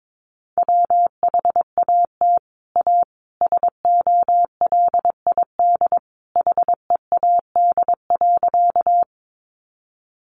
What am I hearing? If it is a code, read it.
Morse code, heard as a keyed tone, about 22 wpm, 700 Hz. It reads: WHAT A SOLID HEAD.
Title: Sentences from Top 1500 Words 22wpm